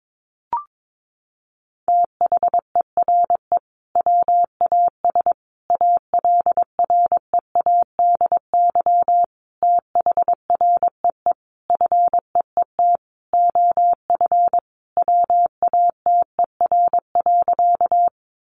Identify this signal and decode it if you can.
Morse code, heard as a keyed tone, about 22 wpm, 700 Hz. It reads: THERE WAS ALREADY THREE FEET OF WATER.
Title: Sentences from Top 700 Words 22wpm